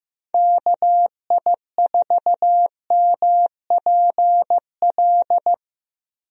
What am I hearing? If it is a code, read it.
Morse code, heard as a keyed tone, about 15 wpm, 700 Hz. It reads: KI4MPL
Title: USA Call Signs - 6 Characters - 15wpm